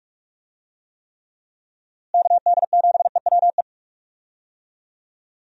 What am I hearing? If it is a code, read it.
Morse code, heard as a keyed tone, about 46 wpm, 700 Hz. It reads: KD7EWE
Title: USA Call Signs - 6 Characters - Repeated 3x - 45wpm